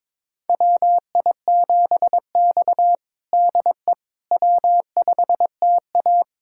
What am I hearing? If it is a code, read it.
Morse code, heard as a keyed tone, about 22 wpm, 700 Hz. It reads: WI7X DE W5TA
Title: USA Callsign DE Callsign - 4 Characters - 22wpm